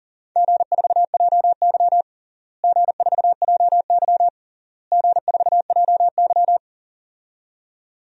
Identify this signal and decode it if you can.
Morse code, heard as a keyed tone, about 37 wpm, 700 Hz. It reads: G4JY G4JY G4JY
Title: DXCC-100 Call Signs - 4 Characters - Repeated 3x - 40wpm